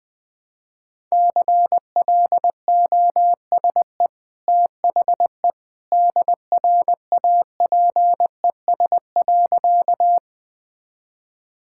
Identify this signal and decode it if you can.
Morse code, heard as a keyed tone, about 21 wpm, 700 Hz. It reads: CLOSE THE DRAPES.
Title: Easy Sentences with Hints - 20wpm